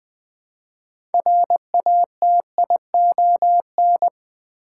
Morse code, heard as a keyed tone, about 20 wpm, 700 Hz. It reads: RATION
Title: Build A Word - 3 Letters At A Time - 8-Letter Words - 20wpm